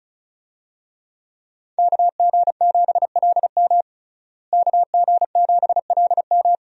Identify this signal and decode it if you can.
Morse code, heard as a keyed tone, about 34 wpm, 700 Hz. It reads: KG7LM KG7LM
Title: USA Call Signs - 5 Characters - Repeated 2x - 35wpm